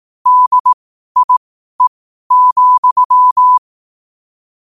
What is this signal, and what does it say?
Morse code, heard as a keyed tone, about 18 wpm, spaced as for 12 wpm, 1000 Hz. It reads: DIE,
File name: qotdfiles / 12wpm